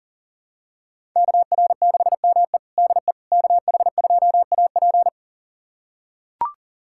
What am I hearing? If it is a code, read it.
Morse code, heard as a keyed tone, about 37 wpm, 700 Hz. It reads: KR6ME DE KH2AP
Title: USA Callsign DE Callsign - 5 Characters - 40wpm